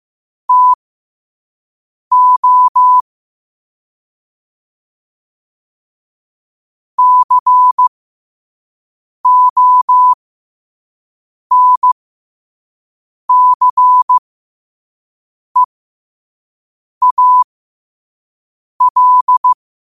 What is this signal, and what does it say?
Morse code, heard as a keyed tone, about 15 wpm, spaced as for 5 wpm, 1000 Hz. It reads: TO CONCEAL
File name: qotdfiles / 5wpm